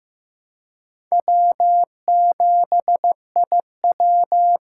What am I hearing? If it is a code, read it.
Morse code, heard as a keyed tone, about 15 wpm, 700 Hz. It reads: W7IW
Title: USA Call Signs - 4 Characters - 2x Character Spacing - 15wpm